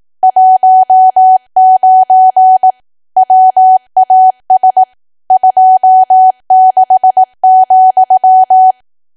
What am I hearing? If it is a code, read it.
Morse code, heard as a keyed tone, about 18 wpm, 750 Hz. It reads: 19 WAS 26,
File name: cwa / W1AW-Level2 / 18WPM